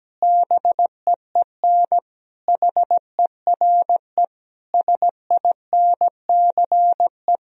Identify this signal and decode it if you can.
Morse code, heard as a keyed tone, about 17 wpm, 700 Hz. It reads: BEEN HERE SINCE
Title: Sentences from Top 1000 Words - Review Part 1 - 17wpm